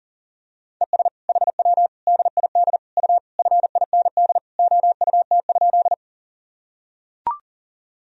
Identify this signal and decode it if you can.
Morse code, heard as a keyed tone, about 40 wpm, 700 Hz. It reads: ES HW DID U FIND OUT?
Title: Sentences from Top 100 Words and Abbreviations 40wpm